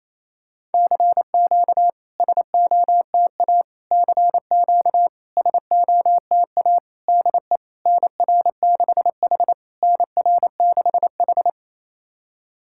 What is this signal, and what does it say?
Morse code, heard as a keyed tone, about 28 wpm, 700 Hz. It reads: CQ SOTA CQ SOTA DE NR6H NR6H
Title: SOTA QSO Practice 28wpm